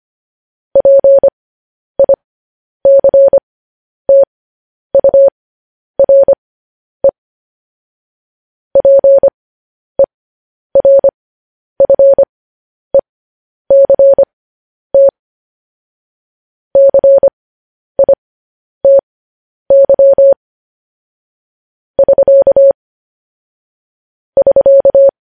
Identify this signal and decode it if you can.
Morse code, heard as a keyed tone, about 25 wpm, spaced as for 10 wpm, 550 Hz. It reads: PICTURE PERFECT CITY <SK> <SK>